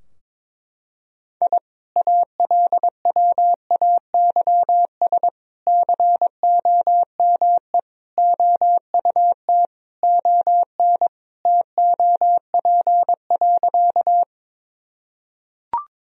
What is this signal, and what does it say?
Morse code, heard as a keyed tone, about 21 wpm, 700 Hz. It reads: I ALWAYS COME OUT ON TOP.
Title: Sentences from Top 1000 Words - Review Part 2 - 22wpm